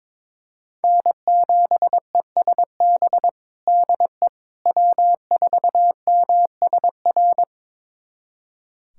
Morse code, heard as a keyed tone, about 21 wpm, 700 Hz. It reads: N7ESB DE W4MSR
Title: USA Callsign DE Callsign - 5 Characters - 22wpm